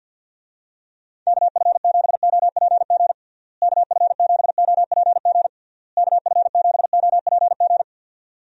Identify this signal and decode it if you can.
Morse code, heard as a keyed tone, about 46 wpm, 700 Hz. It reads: KF7OPG KF7OPG KF7OPG
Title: USA Call Signs - 6 Characters - Repeated 3x - 50wpm